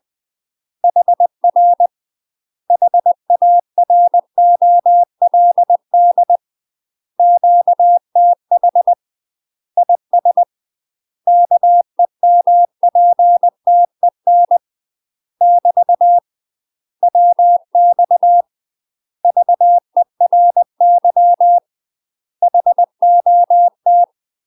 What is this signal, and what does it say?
Morse code, heard as a keyed tone, about 20 wpm, 700 Hz. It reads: HR HAROLD QTH IS KEMPTEN <BT> WX VERY HOT
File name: UK MASTER QSOs TRAINING/20WPM